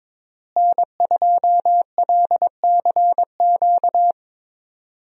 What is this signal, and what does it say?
Morse code, heard as a keyed tone, about 22 wpm, 700 Hz. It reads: N2LCQ
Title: USA Call Signs - 5 Characters - Repeated 2x - 22wpm